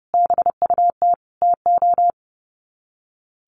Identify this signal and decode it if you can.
Morse code, heard as a keyed tone, about 30 wpm, 700 Hz. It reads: BUT TO
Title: Sets of 5 Words 30wpm